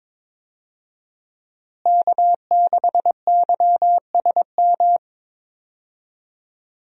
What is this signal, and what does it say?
Morse code, heard as a keyed tone, about 22 wpm, 700 Hz. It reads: K6YSM
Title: USA Call Signs - 5 Characters - Repeated 3x - 22wpm